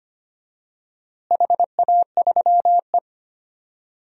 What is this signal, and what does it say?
Morse code, heard as a keyed tone, about 25 wpm, 700 Hz. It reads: HA3E
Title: DXCC-100 Call Signs - 4 Characters - Repeated 2x - 25wpm